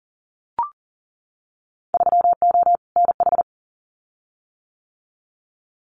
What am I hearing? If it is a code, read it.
Morse code, heard as a keyed tone, about 40 wpm, 700 Hz. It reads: 3O NH
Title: Winter Field Day Exchanges Encore 40wpm